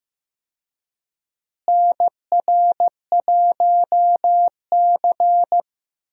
Morse code, heard as a keyed tone, about 15 wpm, 700 Hz. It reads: NR1C
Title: USA Call Signs - 4 Characters - Repeated 3x - 15wpm